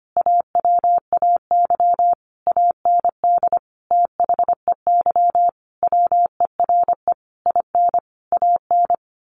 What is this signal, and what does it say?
Morse code, heard as a keyed tone, about 25 wpm, 700 Hz. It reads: AWAY AND THEY WERE IN AN
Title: A Sound Of Thunder 25wpm